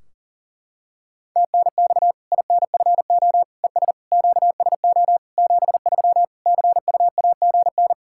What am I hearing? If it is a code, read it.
Morse code, heard as a keyed tone, about 40 wpm, 700 Hz. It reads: TNX INFO ES QSO 73 CUAGN
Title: Practice QSO - Exchange 4 - Ending - 40wpm